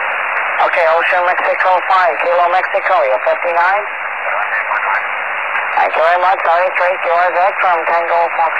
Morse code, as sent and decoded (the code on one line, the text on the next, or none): none